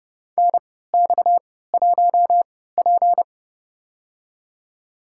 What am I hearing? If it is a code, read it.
Morse code, heard as a keyed tone, about 32 wpm, spaced as for 17 wpm, 700 Hz. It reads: NX1P
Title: USA Call Signs - 4 Characters - 3x Character Spacing - 30wpm